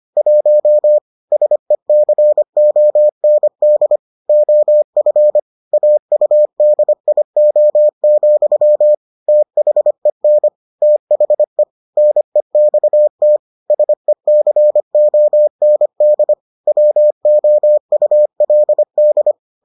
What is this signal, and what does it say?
Morse code, heard as a keyed tone, about 25 wpm, 600 Hz. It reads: 1 SECOND OF AUDIO, THEN THE NEXT SECOND WOULD